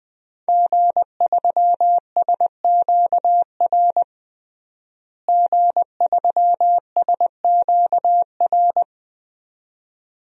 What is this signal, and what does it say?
Morse code, heard as a keyed tone, about 20 wpm, 700 Hz. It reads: G3SQR G3SQR
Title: DXCC-100 Call Signs - 5 Characters - Repeated 2x - 20wpm